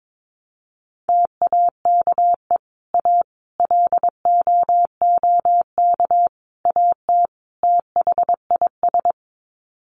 Morse code, heard as a keyed tone, about 22 wpm, 700 Hz. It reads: TAKE A LOOK AT THIS
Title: Sets of 5 Words 22wpm